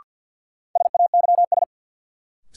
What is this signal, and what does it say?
Morse code, heard as a keyed tone, about 51 wpm, 700 Hz. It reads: SAYS
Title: Sets of 2 Words - New Word Drill - 50wpm